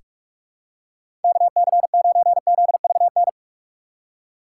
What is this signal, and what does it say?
Morse code, heard as a keyed tone, about 46 wpm, 700 Hz. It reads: KC9ZVN
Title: USA Call Signs - 6 Characters - 3x Character Spacing - 45wpm